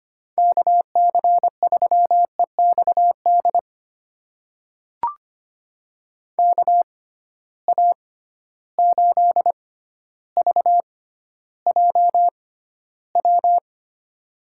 Morse code, heard as a keyed tone, about 25 wpm, spaced as for 13 wpm, 700 Hz. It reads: KC3EXD KA8VJW
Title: USA Call Signs - 6 Characters - 6x Character Spacing - 25wpm